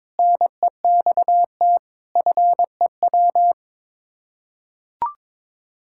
Morse code, heard as a keyed tone, about 22 wpm, 700 Hz. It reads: NEXT FEW
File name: practice / 22wpm / Sets of 7 Words - 22wpm